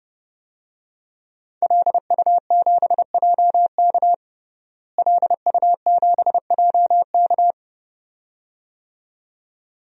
Morse code, heard as a keyed tone, about 30 wpm, 700 Hz. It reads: LU7JK LU7JK
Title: DXCC-100 Call Signs - 5 Characters - Repeated 2x - 30wpm